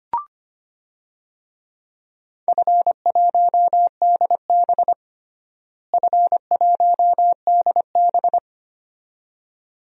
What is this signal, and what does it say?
Morse code, heard as a keyed tone, about 25 wpm, 700 Hz. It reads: F1DB F1DB
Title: DXCC-100 Call Signs - 4 Characters - Repeated 2x - 25wpm